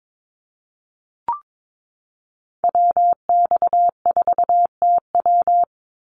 Morse code, heard as a keyed tone, about 22 wpm, 700 Hz. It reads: WX4TW
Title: NA SOTA Activator Callsigns 22wpm